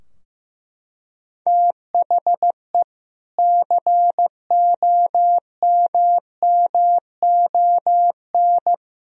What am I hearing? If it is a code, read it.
Morse code, heard as a keyed tone, about 15 wpm, 700 Hz. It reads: THE COMMON
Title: Sentences from Top 1000 Words - Review Part 3 - 15wpm